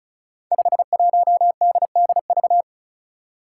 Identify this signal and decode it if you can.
Morse code, heard as a keyed tone, about 36 wpm, 700 Hz. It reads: 51DDV
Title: Five Challenging Letter-Number Combinations 35wpm